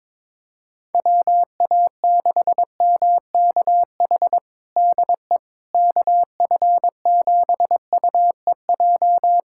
Morse code, heard as a keyed tone, about 22 wpm, 700 Hz. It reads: WA6MKH DE KF7UEJ
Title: K1USN SST QSO Practice 22wpm